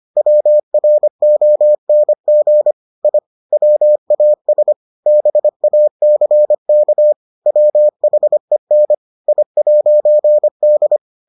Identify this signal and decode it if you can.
Morse code, heard as a keyed tone, about 25 wpm, 600 Hz. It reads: WRONG I WAS BACK WHEN I'D